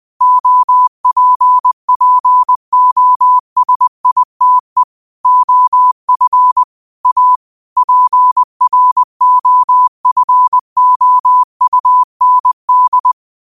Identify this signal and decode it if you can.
Morse code, heard as a keyed tone, about 20 wpm, 1000 Hz. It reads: OPPOSITE OF A PROFOUND
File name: qotdfiles / 20wpm